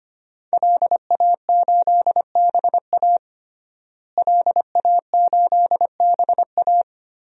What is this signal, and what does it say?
Morse code, heard as a keyed tone, about 25 wpm, 700 Hz. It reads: LA8BA LA8BA
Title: DXCC-100 Call Signs - 5 Characters - Repeated 2x - 25wpm